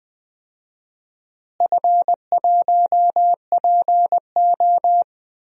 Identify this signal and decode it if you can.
Morse code, heard as a keyed tone, about 20 wpm, 700 Hz. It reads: F1PO
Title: DXCC-100 Call Signs - 4 Characters - Repeated 2x - 20wpm